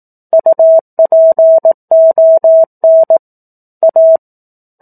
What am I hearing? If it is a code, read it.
Morse code, heard as a keyed tone, about 18 wpm, 650 Hz. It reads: UPON A
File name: 18 WPM